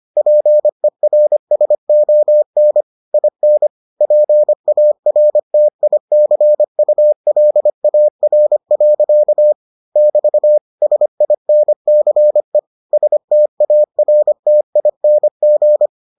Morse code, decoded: PERSON IN PARTICULAR. <BT> SINCE STARTING

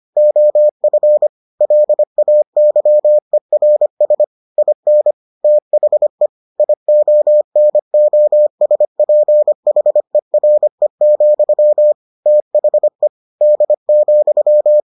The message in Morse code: OF LAYERS IN THE IONOSPHERE, THE D,

